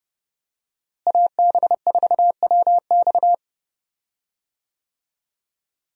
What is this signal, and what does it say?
Morse code, heard as a keyed tone, about 30 wpm, 700 Hz. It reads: AB4WX
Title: USA Call Signs - 5 Characters - Repeated 3x - 30wpm